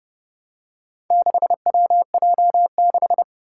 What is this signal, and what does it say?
Morse code, heard as a keyed tone, about 30 wpm, 700 Hz. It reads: 6WJ6
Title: Four Challenging Letter-Number Combinations 30wpm